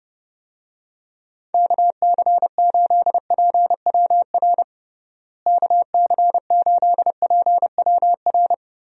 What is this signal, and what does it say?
Morse code, heard as a keyed tone, about 30 wpm, 700 Hz. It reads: KC8PWR KC8PWR
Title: USA Call Signs - 6 Characters - Repeated 3x - 30wpm